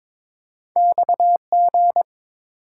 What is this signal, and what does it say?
Morse code, heard as a keyed tone, about 30 wpm, spaced as for 26 wpm, 700 Hz. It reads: XG
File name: Speed-Racing - Two Letter-Number Combinations - 22wpm